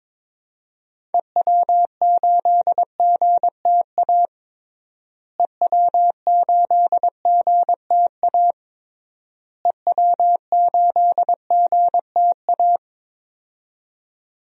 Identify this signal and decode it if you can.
Morse code, heard as a keyed tone, about 22 wpm, 700 Hz. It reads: EW8GTA EW8GTA EW8GTA
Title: DXCC-100 Call Signs - 6 Characters - Repeated 3x - 22wpm